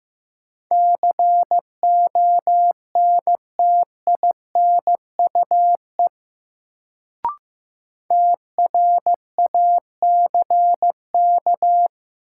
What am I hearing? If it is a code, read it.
Morse code, heard as a keyed tone, about 15 wpm, 700 Hz. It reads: CONTINUE TRACK
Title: Sentences from Top 1100 Words - New Word Drill Alt - 15wpm